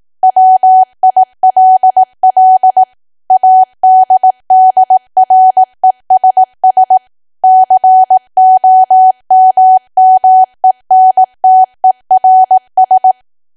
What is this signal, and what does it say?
Morse code, heard as a keyed tone, about 18 wpm, 750 Hz. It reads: WILL ADDRESS COMMENTERS
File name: cwa / W1AW-Level2 / 18WPM